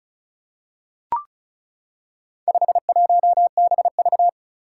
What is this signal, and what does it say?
Morse code, heard as a keyed tone, about 36 wpm, 700 Hz. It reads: 51BV
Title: Four Challenging Letter-Number Combinations 35wpm